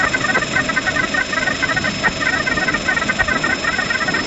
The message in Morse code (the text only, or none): none